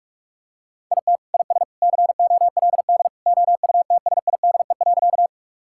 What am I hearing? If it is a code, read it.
Morse code, heard as a keyed tone, about 46 wpm, 700 Hz. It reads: IT IS COLD OUTSIDE.
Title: Sentences from Top 1000 Words - Review Part 6 - 45wpm